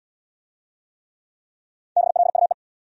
Morse code, heard as a keyed tone, about 69 wpm, 700 Hz. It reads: BLUE